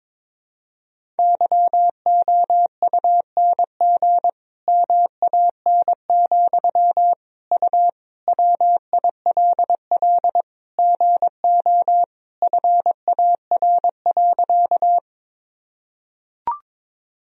Morse code, heard as a keyed tone, about 22 wpm, 700 Hz. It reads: YOUNG MAN, U WILL GO FAR.
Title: Sentences from Top 400 Words and Abbreviations 22wpm